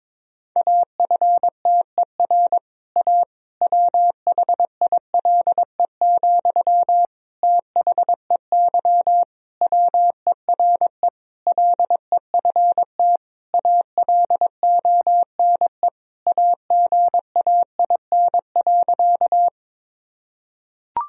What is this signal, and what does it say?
Morse code, heard as a keyed tone, about 22 wpm, 700 Hz. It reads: AFTER A WHILE, THEY WERE LEFT ALONE AGAIN.
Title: Sentences from Top 1000 Words - Review Part 1 - 22wpm